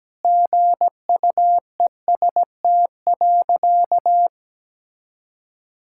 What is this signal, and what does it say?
Morse code, heard as a keyed tone, about 17 wpm, 700 Hz. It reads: GUEST.
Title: Sentences from Top 1600 Words 17wpm